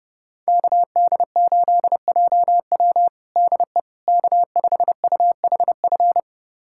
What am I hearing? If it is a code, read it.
Morse code, heard as a keyed tone, about 32 wpm, 700 Hz. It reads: KD8JW DE K5UHF
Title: USA Callsign DE Callsign - 5 Characters - 30wpm